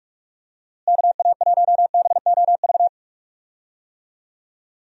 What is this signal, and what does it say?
Morse code, heard as a keyed tone, about 42 wpm, 700 Hz. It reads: KA1BOV
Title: USA Call Signs - 6 Characters - Repeated 3x - 45wpm